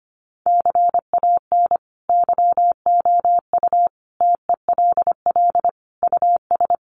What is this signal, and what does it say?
Morse code, heard as a keyed tone, about 25 wpm, 700 Hz. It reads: CAN YOU TELL US
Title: Sentences from Top 1300 Words 25wpm